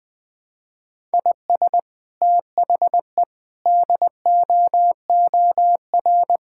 Morse code, heard as a keyed tone, about 20 wpm, 700 Hz. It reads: IS THE DOOR
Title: Sentences with 2-Word Phrases - 20wpm